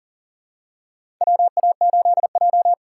Morse code, heard as a keyed tone, about 39 wpm, 700 Hz. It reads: WA8J
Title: USA Call Signs - 4x Character Spacing - 40wpm